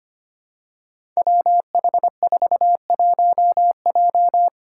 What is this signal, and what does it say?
Morse code, heard as a keyed tone, about 25 wpm, 700 Hz. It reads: WH41J
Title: Five Challenging Letter-Number Combinations 25wpm